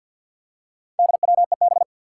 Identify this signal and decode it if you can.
Morse code, heard as a keyed tone, about 46 wpm, 700 Hz. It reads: DWEB
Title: Four Challenging Letter-Number Combinations 50wpm